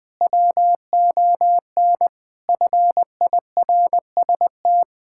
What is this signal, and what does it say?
Morse code, heard as a keyed tone, about 20 wpm, 700 Hz. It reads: WON FIRST
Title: Sentences with 7-Word Phrases - 20wpm